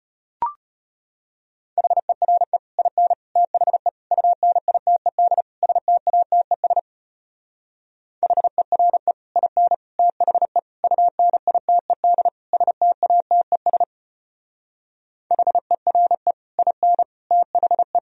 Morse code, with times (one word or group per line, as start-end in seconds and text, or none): 1.8-2.6 HERE
2.8-3.1 IN
3.4-3.9 THE
4.1-5.4 UNITED
5.6-6.8 STATES
8.2-9.1 HERE
9.4-9.7 IN
10.0-10.6 THE
10.8-12.3 UNITED
12.5-13.8 STATES
15.3-16.3 HERE
16.6-17.0 IN
17.3-18.0 THE